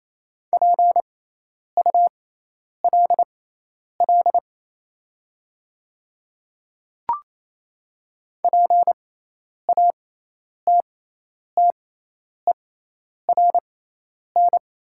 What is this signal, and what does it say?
Morse code, heard as a keyed tone, about 28 wpm, spaced as for 10 wpm, 700 Hz. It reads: PULL PATTERN
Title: 6x Character Spacing - Top 500 Words - 28wpm